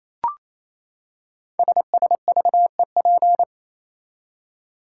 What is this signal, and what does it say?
Morse code, heard as a keyed tone, about 27 wpm, 700 Hz. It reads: SSVEP
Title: Code Groups - 5 Letters - 28wpm